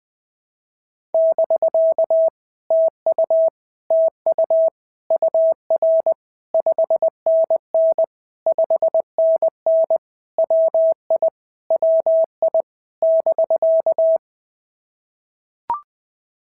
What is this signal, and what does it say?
Morse code, heard as a keyed tone, about 20 wpm, 650 Hz. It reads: <BK> TU TU UR 5NN 5NN WI WI <BK>
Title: POTA QSO Practice - 20wpm